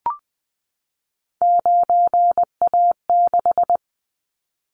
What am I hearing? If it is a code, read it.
Morse code, heard as a keyed tone, about 20 wpm, 700 Hz. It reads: 9A6
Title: Callsign Prefixes for Top 100 DXCC - 20wpm